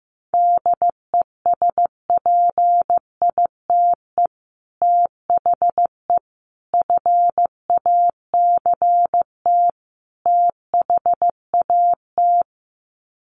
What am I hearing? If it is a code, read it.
Morse code, heard as a keyed tone, about 15 wpm, 700 Hz. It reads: DESPITE THE FACT THAT
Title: Sets of 4 Words 15wpm